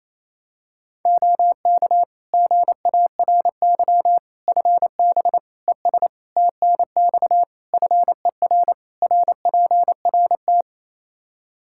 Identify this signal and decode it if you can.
Morse code, heard as a keyed tone, about 28 wpm, 700 Hz. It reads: OK GARY FB ES TNX FER RPRT
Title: Practice QSO - Exchange 2 - Info - 28wpm